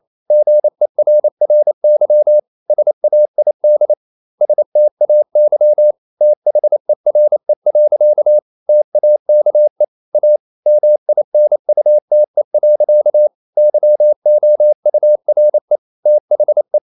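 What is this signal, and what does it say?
Morse code, heard as a keyed tone, about 28 wpm, 600 Hz. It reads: GERRY SAID STAY THERE. TAKE A MINUTE. YOURE THE
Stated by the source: Morse code practice at 28 wpm, text of Ken Follett's novel Never in English, from CW med Gnister.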